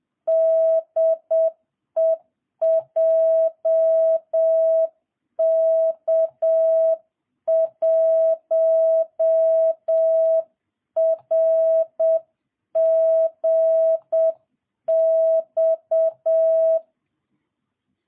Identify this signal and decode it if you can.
Morse code, heard as a keyed tone, about 7 wpm, 660 Hz. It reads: DE JK1RGX